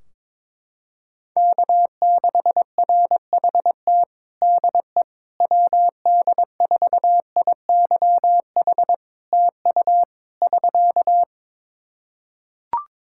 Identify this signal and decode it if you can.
Morse code, heard as a keyed tone, about 22 wpm, 700 Hz. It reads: K6RHT DE WD4IYH TU <SK>
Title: Practice QSO - Exchange 4 - Ending - 22wpm